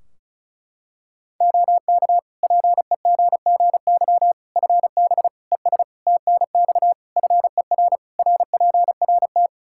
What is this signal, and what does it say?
Morse code, heard as a keyed tone, about 36 wpm, 700 Hz. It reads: OK PEGGY FB ES TNX FER RPRT
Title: Practice QSO - Exchange 2 - Info - 35wpm